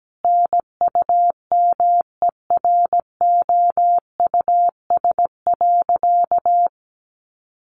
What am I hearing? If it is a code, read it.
Morse code, heard as a keyed tone, about 17 wpm, 700 Hz. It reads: NUMEROUS.
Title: Sentences from Top 1400 Words 17wpm